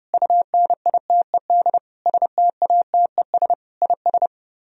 Morse code, heard as a keyed tone, about 32 wpm, 700 Hz. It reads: UNITED STATES IS